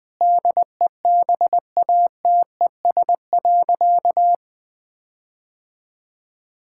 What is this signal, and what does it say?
Morse code, heard as a keyed tone, about 21 wpm, 700 Hz. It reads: DEBATES.
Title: Sentences with 6-Word Phrases - 20wpm